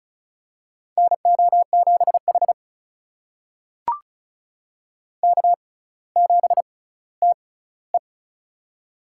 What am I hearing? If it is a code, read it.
Morse code, heard as a keyed tone, about 34 wpm, spaced as for 18 wpm, 700 Hz. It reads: NO7H K7TE
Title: USA Call Signs - 6x Character Spacing - 35wpm